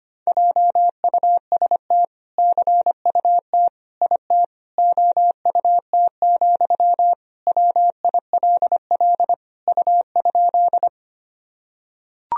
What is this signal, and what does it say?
Morse code, heard as a keyed tone, about 25 wpm, 700 Hz. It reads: JUST CUT IT OUT, WILL U?
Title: Sentences from Top 300 Words and Abbreviations 25wpm